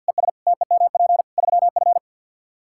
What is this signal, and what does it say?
Morse code, heard as a keyed tone, about 46 wpm, 700 Hz. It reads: ES TEMP 3F
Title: Practice QSO - Exchange 2 - Info - 50wpm